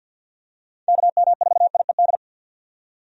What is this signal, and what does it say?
Morse code, heard as a keyed tone, about 46 wpm, 700 Hz. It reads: KM4IED
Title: USA Call Signs - 6 Characters - 50wpm